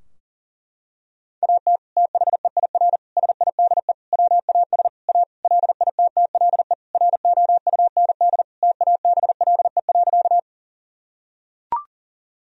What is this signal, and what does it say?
Morse code, heard as a keyed tone, about 40 wpm, 700 Hz. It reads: AT THEIR SIDE WAS A LITTLE ROUND TABLE.
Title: Sentences from Top 1000 Words - Review Part 2 - 40wpm